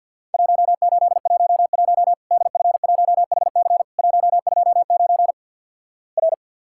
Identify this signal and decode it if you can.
Morse code, heard as a keyed tone, about 51 wpm, 700 Hz. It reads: 1811 DF1HC 129 R
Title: Worked All Europe DX Contest - QTC Exchange - 50wpm